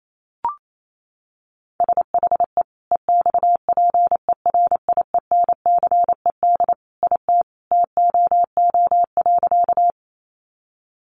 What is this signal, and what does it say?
Morse code, heard as a keyed tone, about 28 wpm, 700 Hz. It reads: SHE EXPERIENCED IT TOO.
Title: Sentences from Top 1100 Words 28wpm